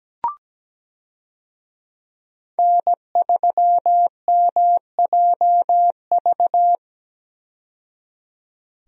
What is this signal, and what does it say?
Morse code, heard as a keyed tone, about 17 wpm, 700 Hz. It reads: N3MJV
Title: USA Call Signs - 5 Characters - 17wpm